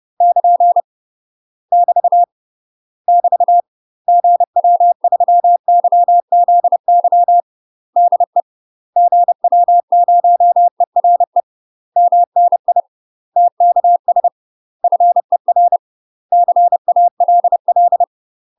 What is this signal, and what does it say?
Morse code, heard as a keyed tone, about 30 wpm, 700 Hz. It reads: <KN> <BT> <BT> GW3YZY DE GW0ERE MNI TKS FER CALL
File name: UK MASTER QSOs TRAINING / 30WPM / UK MASTER QSOs 0031